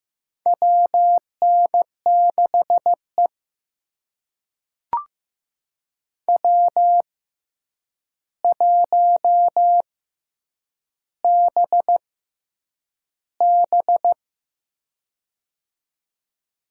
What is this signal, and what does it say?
Morse code, heard as a keyed tone, about 15 wpm, spaced as for 8 wpm, 700 Hz. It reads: WN6E W1BB
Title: USA Call Signs - 4 Characters - 6x Character Spacing - 15wpm